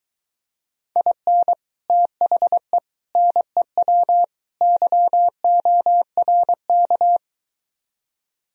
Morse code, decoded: IN THE NEW YORK

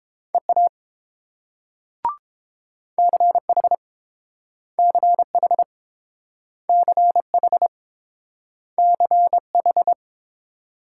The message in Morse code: EA CH CH CH CH

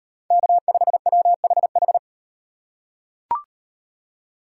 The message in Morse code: K5WHH

